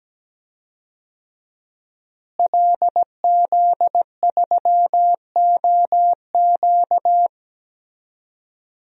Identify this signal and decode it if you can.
Morse code, heard as a keyed tone, about 17 wpm, 700 Hz. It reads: LZ3OQ
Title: DXCC-100 Call Signs - 5 Characters - Repeated 3x - 17wpm